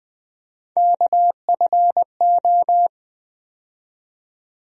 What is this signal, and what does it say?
Morse code, heard as a keyed tone, about 20 wpm, 700 Hz. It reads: KFO
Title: USA Call Sign Postfixes - 3 Letters 20wpm